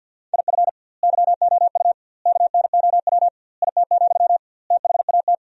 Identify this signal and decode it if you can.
Morse code, heard as a keyed tone, about 50 wpm, 700 Hz. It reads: IF YOU KNOW IT, THAT
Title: 2x Word-Spacing - Sentences from Top 400 Words - 50wpm